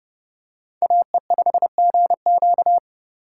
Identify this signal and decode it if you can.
Morse code, heard as a keyed tone, about 27 wpm, 700 Hz. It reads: AE5GQ